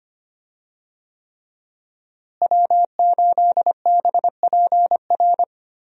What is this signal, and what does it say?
Morse code, heard as a keyed tone, about 25 wpm, 700 Hz. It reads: W8BPR
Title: USA Call Signs - 5 Characters - Repeated 3x - 25wpm